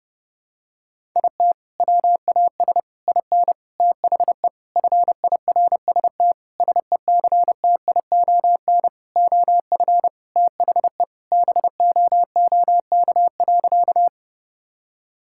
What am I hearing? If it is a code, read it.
Morse code, heard as a keyed tone, about 32 wpm, 700 Hz. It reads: IT WAS IN THE FIRST SECTION OF THE BOOK.
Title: Sentences from Top 1000 Words - Review Part 5 - 30wpm